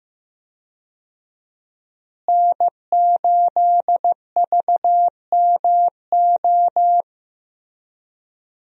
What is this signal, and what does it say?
Morse code, heard as a keyed tone, about 15 wpm, 700 Hz. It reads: N8VMO
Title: USA Call Signs - 5 Characters - Repeated 3x - 15wpm